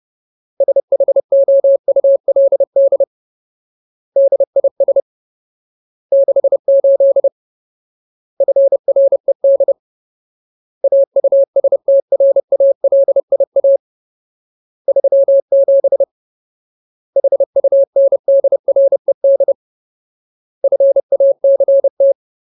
Morse code, heard as a keyed tone, about 30 wpm, 550 Hz. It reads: SHOULD DIS 68 FRED AUSTRALIA 37 HUNDRED FACT
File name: Ultimate Word List / 30WPM Ultimate